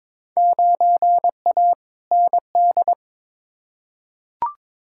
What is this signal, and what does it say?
Morse code, heard as a keyed tone, about 22 wpm, 700 Hz. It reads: 9A ND
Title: Field Day Exchanges Encore - 22wpm